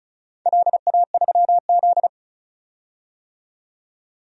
Morse code, heard as a keyed tone, about 36 wpm, 700 Hz. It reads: LA3Z
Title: DXCC-100 Call Signs - 4 Characters - 35wpm